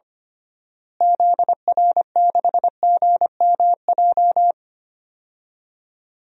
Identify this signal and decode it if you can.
Morse code, heard as a keyed tone, about 25 wpm, 700 Hz. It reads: ZR6GMJ
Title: DXCC-100 Call Signs - 6 Characters - Repeated 3x - 25wpm